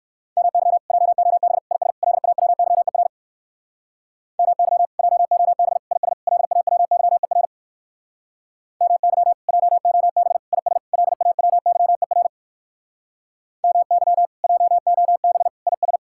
Morse code, heard as a keyed tone, about 61 wpm, 700 Hz. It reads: MY JOB IS LAWYER MY JOB IS LAWYER MY JOB IS LAWYER MY JOB IS